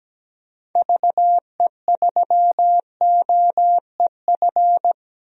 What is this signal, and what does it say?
Morse code, heard as a keyed tone, about 17 wpm, 700 Hz. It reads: VE3OEF
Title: DXCC-100 Call Signs - 6 Characters - 17wpm